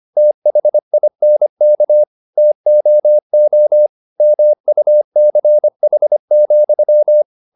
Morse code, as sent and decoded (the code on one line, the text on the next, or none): - .... .. -. -.- / - --- --- / -- ..- -.-. .... --..--
THINK TOO MUCH,